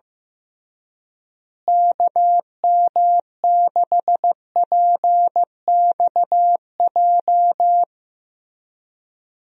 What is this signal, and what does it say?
Morse code, heard as a keyed tone, about 15 wpm, 700 Hz. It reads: KM6PXJ
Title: USA Call Signs - 6 Characters - Repeated 3x - 15wpm